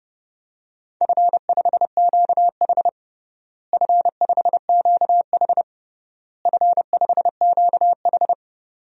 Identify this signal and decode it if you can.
Morse code, heard as a keyed tone, about 30 wpm, 700 Hz. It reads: F5QH F5QH F5QH
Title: DXCC-100 Call Signs - 4 Characters - Repeated 3x - 30wpm